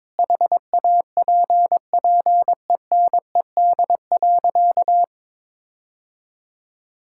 Morse code, heard as a keyed tone, about 22 wpm, 700 Hz. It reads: HAPPENED.
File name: practice / 22wpm / Sentences with 8-Word Phrases - 22wpm